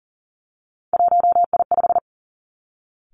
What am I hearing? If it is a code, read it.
Morse code, heard as a keyed tone, about 42 wpm, 700 Hz. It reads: 1I5
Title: Three Challenging Letter-Number Combinations 40wpm